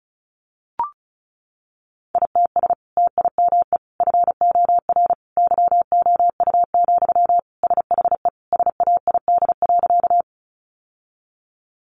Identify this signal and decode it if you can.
Morse code, heard as a keyed tone, about 34 wpm, 700 Hz. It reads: ITS TIME FOR YOU, SHE SAID.
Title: Sentences from Top 600 Words 35wpm